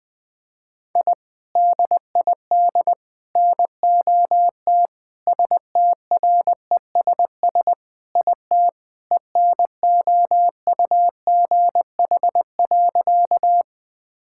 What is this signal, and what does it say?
Morse code, heard as a keyed tone, about 21 wpm, 700 Hz. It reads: I DID NOT STRESS IT ENOUGH.
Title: Sentences from Top 1000 Words - Review Part 6 - 20wpm